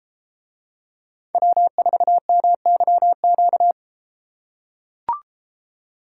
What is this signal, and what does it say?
Morse code, heard as a keyed tone, about 33 wpm, 700 Hz. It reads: W4MYQ